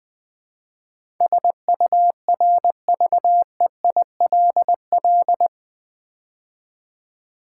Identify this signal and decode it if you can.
Morse code, heard as a keyed tone, about 20 wpm, 700 Hz. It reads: SURVEILL
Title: Build A Word - 4 Letters At A Time - 12-Letter Words - 20wpm